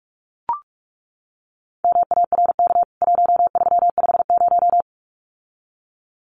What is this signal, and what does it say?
Morse code, heard as a keyed tone, about 44 wpm, 700 Hz. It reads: MARK 1350
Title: CWT Exchange 45wpm